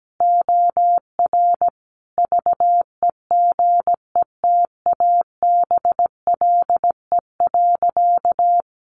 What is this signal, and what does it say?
Morse code, heard as a keyed tone, about 17 wpm, 700 Hz. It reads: OR VEGETABLE.
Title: Sentences from Top 1300 Words 17wpm